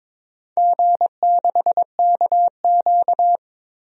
Morse code, heard as a keyed tone, about 22 wpm, 700 Hz. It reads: G6KQ
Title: DXCC-100 Call Signs - 4 Characters - Repeated 3x - 22wpm